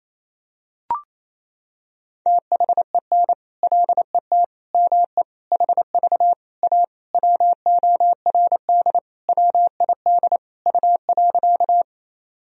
Morse code, heard as a keyed tone, about 28 wpm, 700 Hz. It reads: THEN LET ME HV A WORD WID U.
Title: Sentences from Top 300 Words and Abbreviations 28wpm